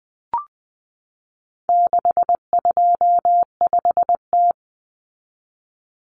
Morse code, heard as a keyed tone, about 20 wpm, 700 Hz. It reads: 625T
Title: Common Cut Numbers - 4-Digit 20wpm